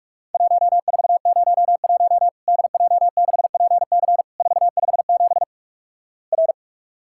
Morse code, heard as a keyed tone, about 46 wpm, 700 Hz. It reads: 1401 DJ6PC 457 R
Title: Worked All Europe DX Contest - QTC Exchange - 45wpm